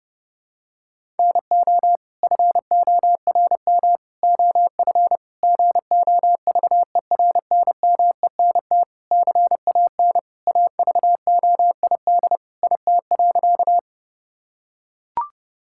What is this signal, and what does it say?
Morse code, heard as a keyed tone, about 30 wpm, 700 Hz. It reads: NO FORM OF GOVERNMENT CAN AVOID IT.
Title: Sentences from Top 1000 Words - Review Part 6 - 30wpm